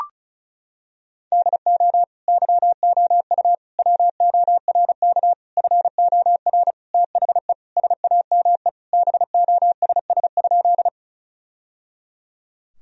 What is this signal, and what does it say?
Morse code, heard as a keyed tone, about 36 wpm, 700 Hz. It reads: DO YOU WORK FOR THE SAME BOSS?
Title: Sentences from Top 1000 Words - Review Part 4 - 35wpm